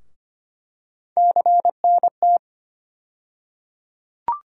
Morse code, heard as a keyed tone, about 25 wpm, 700 Hz. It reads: CNT
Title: USA Call Sign Postfixes - 3 Letters 25wpm